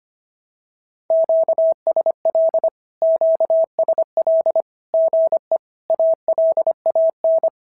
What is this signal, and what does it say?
Morse code, heard as a keyed tone, about 24 wpm, 650 Hz. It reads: QSL QSL GE ALAN
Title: SOTA QSO Practice 25wpm